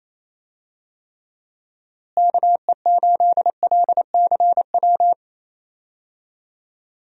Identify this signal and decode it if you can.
Morse code, heard as a keyed tone, about 28 wpm, 700 Hz. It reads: KE8LCW
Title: USA Call Signs - 6 Characters - Repeated 2x - 28wpm